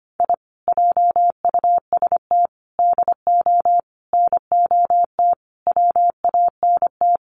Morse code, heard as a keyed tone, about 25 wpm, 700 Hz. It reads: I JUST DO NOT WANT
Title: Sets of 5 Words Encore2 25wpm